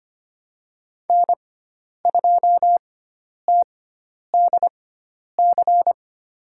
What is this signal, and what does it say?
Morse code, heard as a keyed tone, about 25 wpm, spaced as for 10 wpm, 700 Hz. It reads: N2TDC
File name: Lesson 16g - Letter c - Call Signs 25wpm